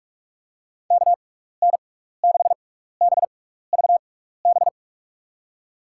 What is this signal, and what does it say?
Morse code, heard as a keyed tone, about 46 wpm, spaced as for 15 wpm, 700 Hz. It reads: KN6BVB
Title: USA Call Signs - 6x Character Spacing - 45wpm